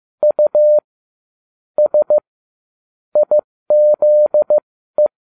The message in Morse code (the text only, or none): U S IZE